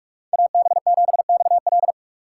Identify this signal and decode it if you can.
Morse code, heard as a keyed tone, about 46 wpm, 700 Hz. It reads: AB7XL